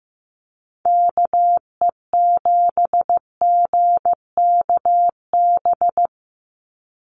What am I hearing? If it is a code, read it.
Morse code, heard as a keyed tone, about 15 wpm, 700 Hz. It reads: KE7GKB
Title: USA Call Signs 15wpm